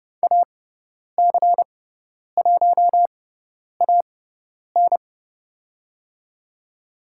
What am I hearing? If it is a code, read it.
Morse code, heard as a keyed tone, about 30 wpm, spaced as for 10 wpm, 700 Hz. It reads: AC1AN
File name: Lesson 18g - Letter m - Call Signs 30wpm